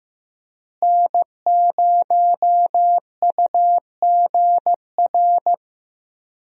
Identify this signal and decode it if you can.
Morse code, heard as a keyed tone, about 15 wpm, 700 Hz. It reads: N0UGR
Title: USA Call Signs - 5 Characters - Repeated 3x - 15wpm